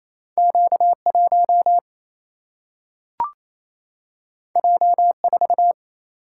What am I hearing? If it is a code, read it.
Morse code, heard as a keyed tone, about 28 wpm, 700 Hz. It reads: Q 1 J 4
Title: Single-Letter Single-Number 28wpm